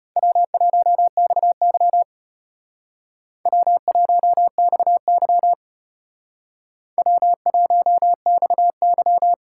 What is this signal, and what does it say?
Morse code, heard as a keyed tone, about 32 wpm, 700 Hz. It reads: W1XY W1XY W1XY